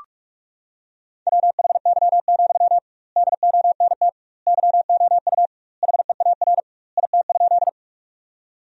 Morse code, heard as a keyed tone, about 41 wpm, 700 Hz. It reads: WHY, DONT YOU HEAR IT?
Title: 2x Word-Spacing - Sentences from Top 300 Words - 45wpm